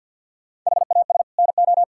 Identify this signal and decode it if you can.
Morse code, heard as a keyed tone, about 50 wpm, 700 Hz. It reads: HAS NO
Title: Sentences from Top 1600 Words 50wpm